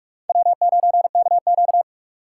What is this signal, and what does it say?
Morse code, heard as a keyed tone, about 46 wpm, 700 Hz. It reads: W9KQ
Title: USA Call Signs - 4 Characters - 4x Character Spacing - 45wpm